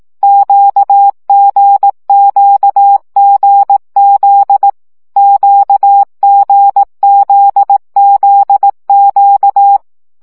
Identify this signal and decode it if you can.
Morse code, heard as a keyed tone, about 18 wpm, 800 Hz. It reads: QGQGZ QGZZQ
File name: CW gyakorlas g z q 15 baud